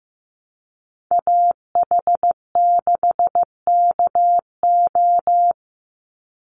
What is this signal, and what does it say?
Morse code, heard as a keyed tone, about 15 wpm, 700 Hz. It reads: AH6KO
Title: Call Signs Repeated 3x - 15wpm